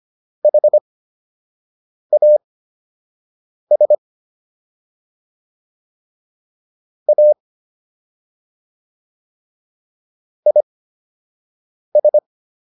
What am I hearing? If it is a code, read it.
Morse code, heard as a keyed tone, about 25 wpm, spaced as for 6 wpm, 600 Hz. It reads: HAS A IS